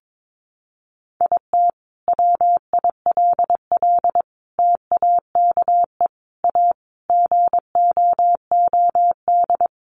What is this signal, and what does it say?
Morse code, heard as a keyed tone, about 22 wpm, 700 Hz. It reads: IT WILL TAKE A GOOD
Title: Sentences from Top 1300 Words 22wpm